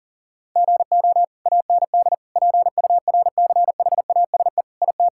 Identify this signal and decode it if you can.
Morse code, heard as a keyed tone, about 40 wpm, 700 Hz. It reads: GO AND PURCHASE IT
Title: Sentences from Top 1000 Words - Review Part 5 - 40wpm